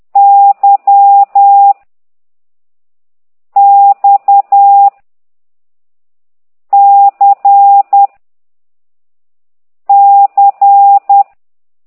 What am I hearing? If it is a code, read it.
Morse code, heard as a keyed tone, about 10 wpm, 800 Hz. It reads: Y X C C